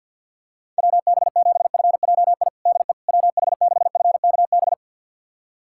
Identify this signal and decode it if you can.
Morse code, heard as a keyed tone, about 51 wpm, 700 Hz. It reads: WB7FJI DE WH6FKB
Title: USA Callsign DE Callsign - 6 Characters - 50wpm